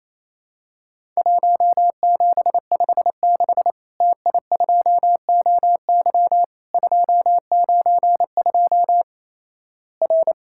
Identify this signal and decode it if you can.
Morse code, heard as a keyed tone, about 28 wpm, 700 Hz. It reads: 1756 TI2OY 292 R
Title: Worked All Europe DX Contest - QTC Exchange - 28wpm